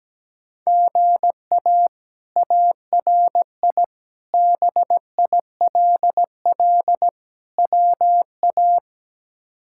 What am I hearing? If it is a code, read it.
Morse code, heard as a keyed tone, about 17 wpm, 700 Hz. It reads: GA ARI BILL WA
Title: K1USN SST QSO Practice - 17wpm